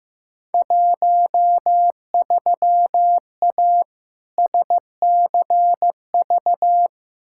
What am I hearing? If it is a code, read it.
Morse code, heard as a keyed tone, about 16 wpm, 700 Hz. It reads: 13A SCV
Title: Field Day Exchanges Encore - 15wpm